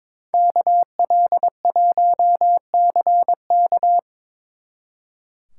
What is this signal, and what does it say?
Morse code, heard as a keyed tone, about 25 wpm, 700 Hz. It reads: KL1CK
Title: Speed Racing - USA 2x2 Call Signs 22wpm